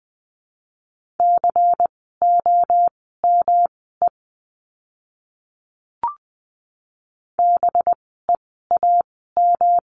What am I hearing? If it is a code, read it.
Morse code, heard as a keyed tone, about 20 wpm, spaced as for 15 wpm, 700 Hz. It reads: COME BEAM
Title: 185 Lesson 30f - Letter b - Words 20wpm